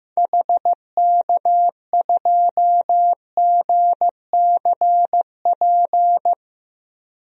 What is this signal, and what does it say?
Morse code, heard as a keyed tone, about 16 wpm, 700 Hz. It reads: HK2GCP
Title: DXCC-100 Call Signs - 6 Characters - 15wpm